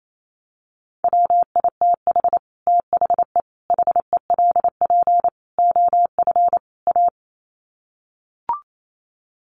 Morse code, heard as a keyed tone, about 28 wpm, 700 Hz. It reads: WITH THE HELP OF A
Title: Sets of 5 Words 28wpm